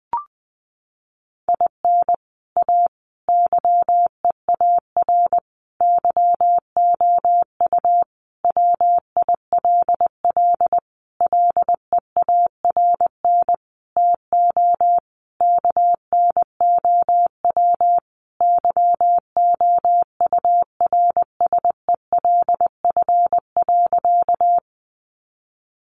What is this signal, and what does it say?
Morse code, heard as a keyed tone, about 19 wpm, 700 Hz. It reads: IN A YEAR YOU WILL LEARN TO KNOW YOURSELF.
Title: Sentences from Top 600 Words 20wpm